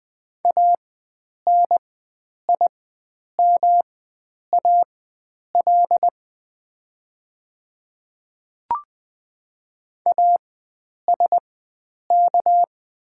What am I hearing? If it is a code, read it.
Morse code, heard as a keyed tone, about 20 wpm, spaced as for 9 wpm, 700 Hz. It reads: ANIMAL ASK
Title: 4x Character Spacing - Top 500 Words - 20wpm